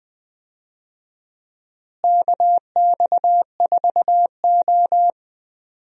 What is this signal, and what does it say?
Morse code, heard as a keyed tone, about 21 wpm, 700 Hz. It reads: KX4O
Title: USA Call Signs - 4 Characters - Repeated 2x - 20wpm